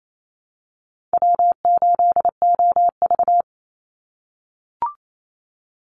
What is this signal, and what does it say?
Morse code, heard as a keyed tone, about 28 wpm, 700 Hz. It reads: W8OV
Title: Easy Call Signs - 28wpm